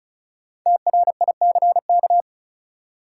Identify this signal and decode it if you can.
Morse code, heard as a keyed tone, about 34 wpm, 700 Hz. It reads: TRICK
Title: Sentences from Top 1500 Words - New Word Drill Alt - 35wpm